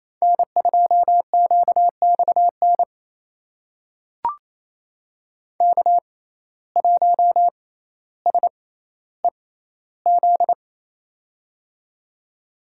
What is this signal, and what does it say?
Morse code, heard as a keyed tone, about 28 wpm, spaced as for 14 wpm, 700 Hz. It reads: N2QXN K1SEZ
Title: USA Call Signs - 5 Characters - 6x Character Spacing - 28wpm